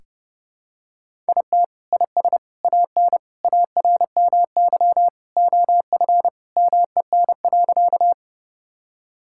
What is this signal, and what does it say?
Morse code, heard as a keyed tone, about 30 wpm, 700 Hz. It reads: IT IS AN ARMY OF MEN.
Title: Sentences from Top 1000 Words - Review Part 5 - 30wpm